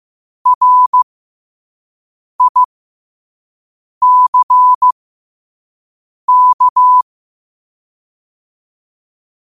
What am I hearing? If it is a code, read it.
Morse code, heard as a keyed tone, about 15 wpm, spaced as for 5 wpm, 1000 Hz. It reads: RICK
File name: qotdfiles / 5wpm